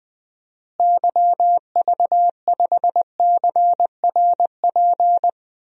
Morse code, heard as a keyed tone, about 20 wpm, 700 Hz. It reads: YV5CRP
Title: DXCC-100 Call Signs - 6 Characters - Repeated 3x - 20wpm